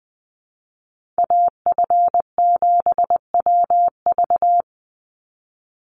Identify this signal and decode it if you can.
Morse code, heard as a keyed tone, about 20 wpm, 700 Hz. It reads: AF7WV
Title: USA Call Signs 20wpm